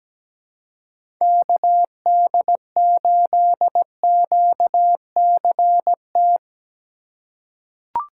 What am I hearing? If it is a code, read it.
Morse code, heard as a keyed tone, about 17 wpm, 700 Hz. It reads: KD8QCT
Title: USA Call Signs - 6 Characters - 4x Character Spacing - 17wpm